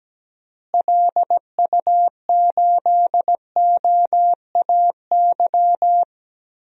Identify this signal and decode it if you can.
Morse code, heard as a keyed tone, about 17 wpm, 700 Hz. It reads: LU8OAY
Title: DXCC-100 Call Signs - 6 Characters - Repeated 3x - 17wpm